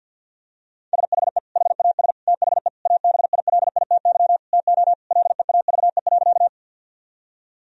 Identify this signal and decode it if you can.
Morse code, heard as a keyed tone, about 50 wpm, 700 Hz. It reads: SHE HAS THE ABILITY TO LEAVE.
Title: Sentences from Top 1000 Words - Review Part 3 - 50wpm